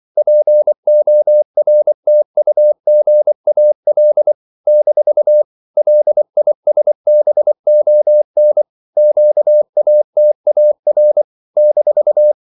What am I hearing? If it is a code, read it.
Morse code, heard as a keyed tone, about 24 wpm, 600 Hz. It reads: PORTUGAL - LISBON QATAR -
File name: CW / Practice / NationalCapitals24